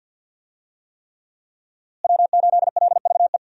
Morse code, heard as a keyed tone, about 45 wpm, 700 Hz. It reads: W8LVE
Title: USA Call Signs - 5 Characters - Repeated 3x - 50wpm